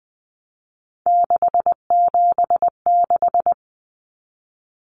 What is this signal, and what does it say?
Morse code, heard as a keyed tone, about 20 wpm, 700 Hz. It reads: 676
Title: Numbers - 3-Digit 20wpm